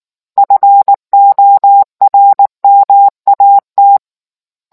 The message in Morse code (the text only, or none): FORMAT